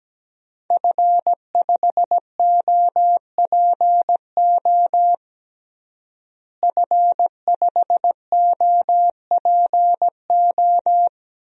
Morse code, heard as a keyed tone, about 17 wpm, 700 Hz. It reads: F5OPO F5OPO
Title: DXCC-100 Call Signs - 5 Characters - Repeated 3x - 17wpm